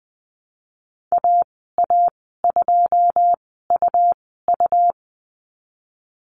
Morse code, heard as a keyed tone, about 20 wpm, spaced as for 15 wpm, 700 Hz. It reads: AA2UU
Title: 161 Lesson 26g - Letter g - Call Signs 20wpm